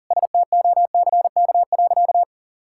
Morse code, heard as a keyed tone, about 40 wpm, 700 Hz. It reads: STOCK.
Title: Sentences from Top 1000 Words - Review Part 5 - 40wpm